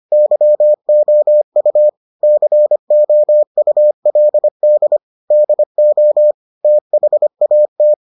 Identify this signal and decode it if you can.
Morse code, heard as a keyed tone, about 25 wpm, 600 Hz. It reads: YOU COULD DO THAT